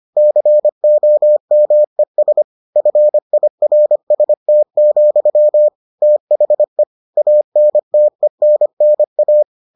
Morse code, heard as a keyed tone, about 25 wpm, 600 Hz. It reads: COMES FIRST, THE ANTENNA